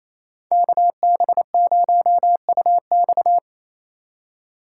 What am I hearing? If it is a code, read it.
Morse code, heard as a keyed tone, about 28 wpm, 700 Hz. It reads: KB0UX
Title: USA Call Signs - 5 Characters - 2x Character Spacing - 28wpm